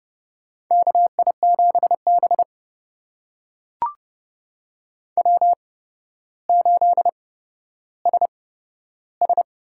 Morse code, heard as a keyed tone, about 30 wpm, spaced as for 13 wpm, 700 Hz. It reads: KI7B W8SS
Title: USA Call Signs - 4 Characters - 8x Character Spacing - 30wpm